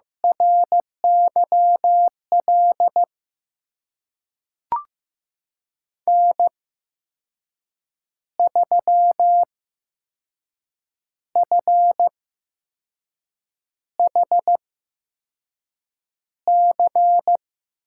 Morse code, heard as a keyed tone, about 15 wpm, spaced as for 5 wpm, 700 Hz. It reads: RYL N3FHC
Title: USA Call Signs - 5 Characters - 8x Character Spacing - 15wpm